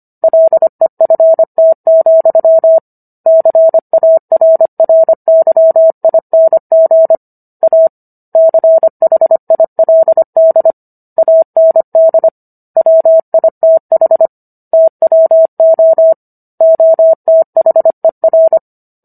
Morse code, heard as a keyed tone, about 25 wpm, 650 Hz. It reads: LEFT, CARRYING A CHILD AND WITH TWO OTHER